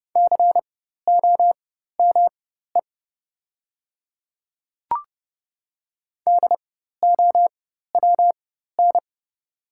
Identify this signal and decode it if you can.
Morse code, heard as a keyed tone, about 30 wpm, spaced as for 14 wpm, 700 Hz. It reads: COME DOWN
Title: 4x Character Spacing - Top 100 Words - 30wpm